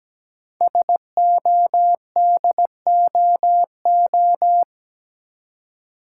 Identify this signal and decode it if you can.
Morse code, heard as a keyed tone, about 17 wpm, 700 Hz. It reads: SODOO
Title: Code Groups - 5 Letters - 17wpm